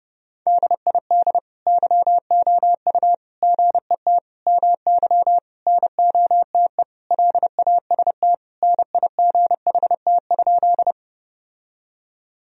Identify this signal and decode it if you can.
Morse code, heard as a keyed tone, about 30 wpm, 700 Hz. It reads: DID YOU GET MY NOTE LAST NIGHT?
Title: Sentences from Top 1000 Words - Review Part 2 - 30wpm